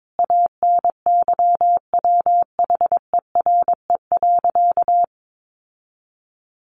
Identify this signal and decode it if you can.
Morse code, heard as a keyed tone, about 22 wpm, 700 Hz. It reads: ANYWHERE.
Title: Sentences from Top 1400 Words 22wpm